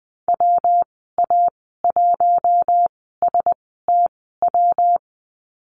Morse code, heard as a keyed tone, about 20 wpm, spaced as for 15 wpm, 700 Hz. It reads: WA1STW
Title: 193 Lesson 31g - Letter v - Call Signs 20wpm